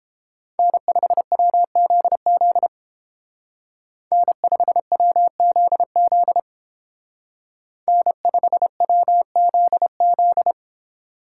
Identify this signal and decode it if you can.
Morse code, heard as a keyed tone, about 28 wpm, 700 Hz. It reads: N5WZZ N5WZZ N5WZZ